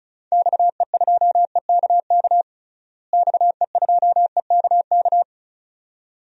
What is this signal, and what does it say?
Morse code, heard as a keyed tone, about 34 wpm, 700 Hz. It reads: XE2EKK XE2EKK
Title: DXCC-100 Call Signs - 6 Characters - Repeated 2x - 35wpm